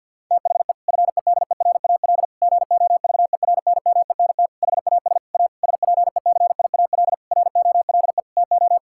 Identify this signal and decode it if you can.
Morse code, heard as a keyed tone, about 53 wpm, 700 Hz. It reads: THE FEDERAL GOVERNMENT HAS A SPECIAL ROLE TO
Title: Sentences from Top 1000 Words - Review Part 6 - 50wpm